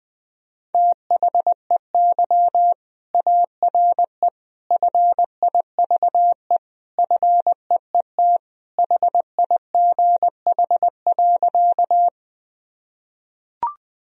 Morse code, heard as a keyed tone, about 20 wpm, 700 Hz. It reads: THEY ARE FIVE FEET HIGH.
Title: Sentences from Top 1000 Words - Review Part 2 - 20wpm